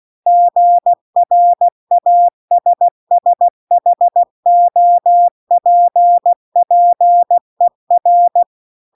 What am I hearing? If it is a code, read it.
Morse code, heard as a keyed tone, about 16 wpm, 700 Hz. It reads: GRASSHOPPER
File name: 16WPM